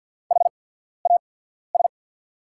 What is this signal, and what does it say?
Morse code, heard as a keyed tone, about 50 wpm, spaced as for 14 wpm, 700 Hz. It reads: HAS
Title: 8x Character Spacing - Top 100 Words - 50wpm